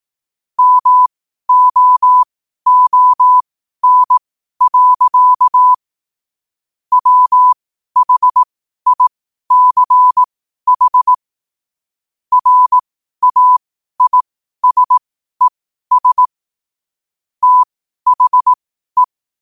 Morse code, heard as a keyed tone, about 18 wpm, spaced as for 12 wpm, 1000 Hz. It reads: MOON. WHICH RAISES THE
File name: qotdfiles / 12wpm